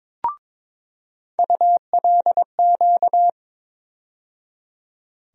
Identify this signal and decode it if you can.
Morse code, heard as a keyed tone, about 21 wpm, 700 Hz. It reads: ULQ
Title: USA Call Sign Postfixes - 3 Letters 22wpm